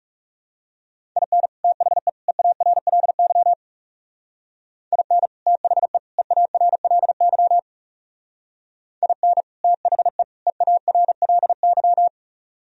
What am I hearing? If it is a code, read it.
Morse code, heard as a keyed tone, about 35 wpm, 700 Hz. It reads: IN THE EARLY IN THE EARLY IN THE EARLY
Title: Speed-Racing - Sets of 3 Words 30wpm